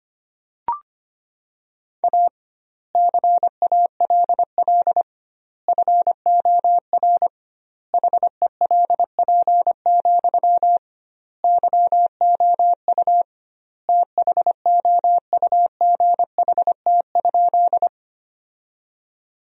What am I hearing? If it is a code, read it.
Morse code, heard as a keyed tone, about 25 wpm, 700 Hz. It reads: A CALL FOR HELP, YOU THOUGHT?
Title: 2x Word-Spacing - Sentences from Top 300 Words - 25wpm